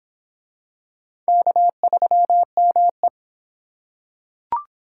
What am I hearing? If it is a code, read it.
Morse code, heard as a keyed tone, about 25 wpm, 700 Hz. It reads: K3ME